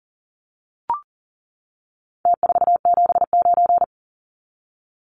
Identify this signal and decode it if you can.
Morse code, heard as a keyed tone, about 40 wpm, 700 Hz. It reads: T479
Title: Numbers - Contest Style - 4-Digit 40wpm